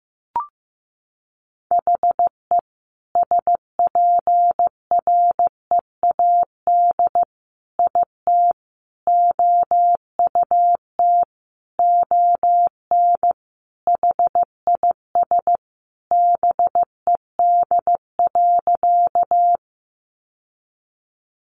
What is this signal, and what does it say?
Morse code, heard as a keyed tone, about 15 wpm, 700 Hz. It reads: HE SPREAD IT OUT ON HIS BED.
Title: Sentences from Top 1200 Words 15wpm